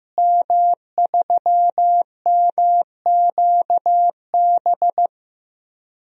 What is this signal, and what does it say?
Morse code, heard as a keyed tone, about 15 wpm, 700 Hz. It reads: M3MQB
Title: DXCC-100 Call Signs - 5 Characters - Repeated 3x - 15wpm